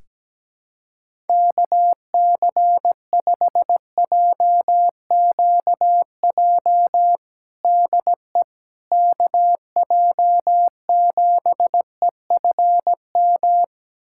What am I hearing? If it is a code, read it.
Morse code, heard as a keyed tone, about 17 wpm, 700 Hz. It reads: KC5JQJ DE KJ7EFM
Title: USA Callsign DE Callsign - 6 Characters - 17wpm